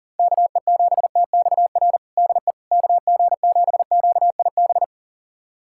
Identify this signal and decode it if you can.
Morse code, heard as a keyed tone, about 40 wpm, 700 Hz. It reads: KE7TXR DE KG7QIB
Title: USA Callsign DE Callsign - 6 Characters - 40wpm